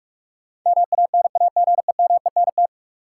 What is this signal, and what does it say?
Morse code, heard as a keyed tone, about 42 wpm, 700 Hz. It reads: MANAGEMENT
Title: Sentences from Top 1000 Words - Review Part 4 - New Word Drill - 45wpm